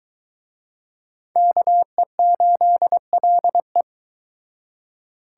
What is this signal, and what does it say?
Morse code, heard as a keyed tone, about 23 wpm, 700 Hz. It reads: KE8LE